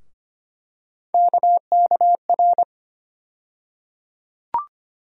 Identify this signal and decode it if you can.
Morse code, heard as a keyed tone, about 25 wpm, 700 Hz. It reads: KKR
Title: USA Call Sign Postfixes - 3 Letters 25wpm